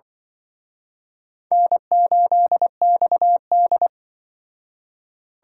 Morse code, heard as a keyed tone, about 24 wpm, 700 Hz. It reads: N8XD
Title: Speed Racing - USA Call Signs 20wpm